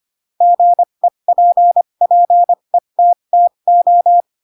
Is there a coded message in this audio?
Yes: Morse code from 0.4 to 4.2 s, heard as a keyed tone, about 25 wpm, spaced as for 22 wpm, 700 Hz. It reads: GEPPETTO